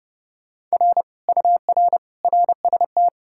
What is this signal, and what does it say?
Morse code, heard as a keyed tone, about 37 wpm, 700 Hz. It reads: R UR RST
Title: Speed-Racing Common QSO Phrases - 30wpm